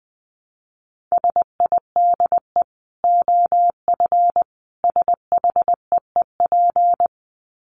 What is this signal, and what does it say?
Morse code, heard as a keyed tone, about 20 wpm, 700 Hz. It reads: SIDE OF SHEEP
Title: Dit Challenge 20wpm